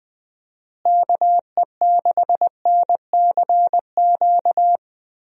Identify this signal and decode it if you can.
Morse code, heard as a keyed tone, about 20 wpm, 700 Hz. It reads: KE6NCQ
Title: USA Call Signs - 6 Characters - Repeated 2x - 20wpm